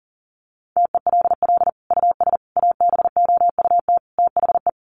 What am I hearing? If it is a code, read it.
Morse code, heard as a keyed tone, about 40 wpm, 700 Hz. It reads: TELL US ABOUT THE
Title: Sentences from Top 1200 Words 40wpm